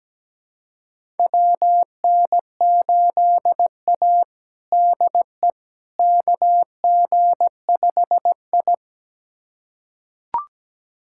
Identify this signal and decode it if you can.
Morse code, heard as a keyed tone, about 17 wpm, 700 Hz. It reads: WN8A DE KG5I
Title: USA Callsign DE Callsign - 4 Characters - 17wpm